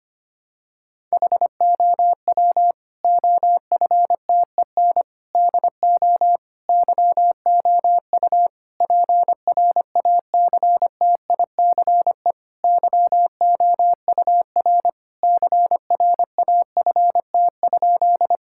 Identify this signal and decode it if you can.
Morse code, heard as a keyed tone, about 25 wpm, 700 Hz. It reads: HOW OFTEN DO YOU PRACTICE YOUR CRAFT?
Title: Sentences from Top 1000 Words - Review Part 6 - 25wpm